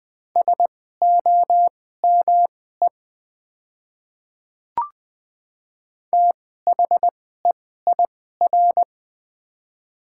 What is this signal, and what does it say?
Morse code, heard as a keyed tone, about 20 wpm, spaced as for 14 wpm, 700 Hz. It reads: SOME THEIR
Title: 2x Character Spacing - Top 200 Words - 20wpm